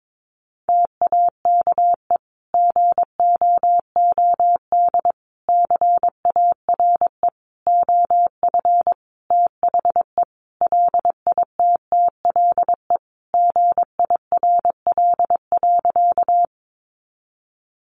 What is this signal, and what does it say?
Morse code, heard as a keyed tone, about 22 wpm, 700 Hz. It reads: TAKE GOOD CARE OF THE LITTLE GIRL.
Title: Sentences from Top 400 Words 22wpm